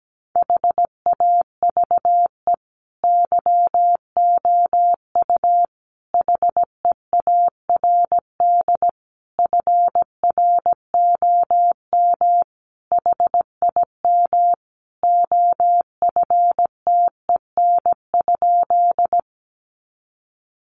Morse code, heard as a keyed tone, about 17 wpm, 700 Hz. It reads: HAVE YOU HEARD FROM HIM OFTEN?
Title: Sentences from Top 400 Words 17wpm